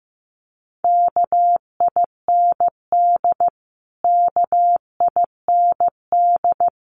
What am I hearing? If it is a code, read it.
Morse code, heard as a keyed tone, about 15 wpm, 700 Hz. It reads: KIND KIND
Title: Top 300 Words Repeated 15wpm